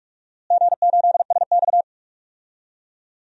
Morse code, heard as a keyed tone, about 46 wpm, 700 Hz. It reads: G8SX
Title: DXCC-100 Call Signs - 4 Characters - Repeated 2x - 45wpm